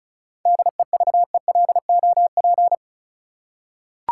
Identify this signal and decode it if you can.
Morse code, heard as a keyed tone, about 34 wpm, 700 Hz. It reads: DEVELOP
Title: Seven Letter Words - 35wpm